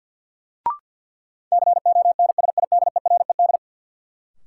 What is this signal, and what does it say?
Morse code, heard as a keyed tone, about 51 wpm, 700 Hz. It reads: CONSIDERED
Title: Sets of 5 Words - New Word Drill - 50wpm